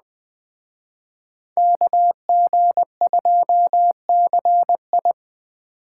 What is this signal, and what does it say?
Morse code, heard as a keyed tone, about 20 wpm, 700 Hz. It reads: KG2CI